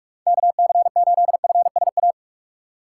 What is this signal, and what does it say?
Morse code, heard as a keyed tone, about 42 wpm, 700 Hz. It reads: KC8FSA
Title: USA Call Signs - 4x Character Spacing - 45wpm